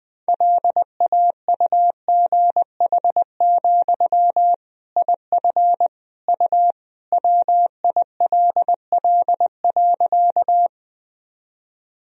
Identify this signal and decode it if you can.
Morse code, heard as a keyed tone, about 20 wpm, 700 Hz. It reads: LAUGH, IF U WILL.
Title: Sentences from Top 500 Words and Abbreviations 20wpm